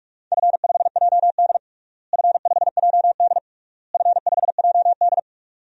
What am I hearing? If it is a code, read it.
Morse code, heard as a keyed tone, about 46 wpm, 700 Hz. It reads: F5JD F5JD F5JD
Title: DXCC-100 Call Signs - 4 Characters - Repeated 3x - 45wpm